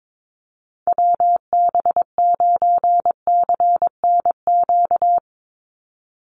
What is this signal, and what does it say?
Morse code, heard as a keyed tone, about 22 wpm, 700 Hz. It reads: WB9CNQ
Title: USA Call Signs 22wpm